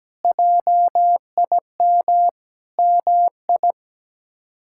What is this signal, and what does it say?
Morse code, heard as a keyed tone, about 17 wpm, 700 Hz. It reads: JIM MI
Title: K1USN SST QSO Practice - 17wpm